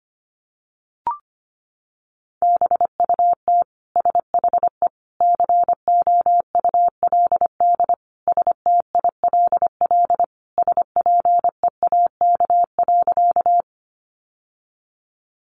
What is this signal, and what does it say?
Morse code, heard as a keyed tone, about 25 wpm, 700 Hz. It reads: BUT SHE COULD STILL SPEAK.
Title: Sentences from Top 1100 Words 25wpm